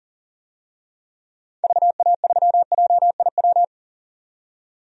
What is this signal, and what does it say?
Morse code, heard as a keyed tone, about 40 wpm, 700 Hz. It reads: VA3JIW
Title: DXCC-100 Call Signs - 6 Characters - 40wpm